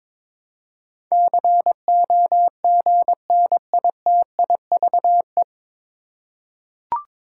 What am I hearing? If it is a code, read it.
Morse code, heard as a keyed tone, about 22 wpm, 690 Hz. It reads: COGNITIVE
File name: 9 Letter Words - First 3 Letters Given - 22wpm